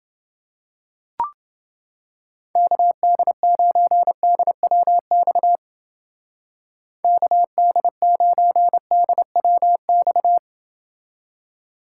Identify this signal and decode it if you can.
Morse code, heard as a keyed tone, about 29 wpm, 700 Hz. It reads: KD9DWX KD9DWX